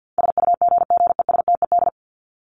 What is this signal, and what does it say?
Morse code, heard as a keyed tone, about 50 wpm, 700 Hz. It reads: SUGGESTED
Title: ED Suffix - Top 100 Words with Base Words 50wpm